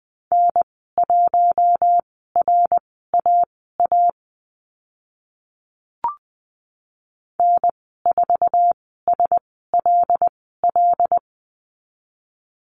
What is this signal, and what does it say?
Morse code, heard as a keyed tone, about 20 wpm, spaced as for 14 wpm, 700 Hz. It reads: N1RAA N4SLL
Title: 076 Lesson 13g - Letter L - Call Signs 20wpm